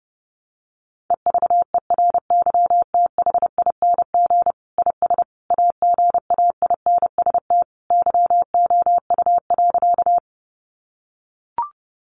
Morse code, heard as a keyed tone, about 30 wpm, 700 Hz. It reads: EVERYTHING IS AGAINST YOU.
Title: Sentences from Top 700 Words 30wpm